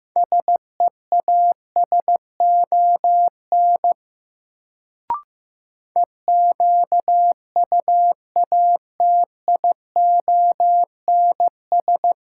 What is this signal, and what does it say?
Morse code, heard as a keyed tone, about 15 wpm, 700 Hz. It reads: SEASON EQUATIONS
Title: Sets of 5 Words - New Word Drill - 15wpm